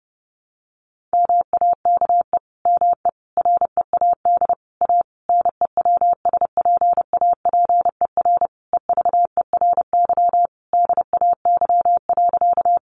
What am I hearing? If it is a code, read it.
Morse code, heard as a keyed tone, about 30 wpm, 700 Hz. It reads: MAKE ME READ A NEWSPAPER EVERY DAY.
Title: Sentences from Top 1100 Words 30wpm